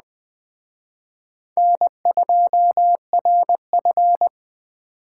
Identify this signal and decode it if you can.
Morse code, heard as a keyed tone, about 20 wpm, 700 Hz. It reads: N2RF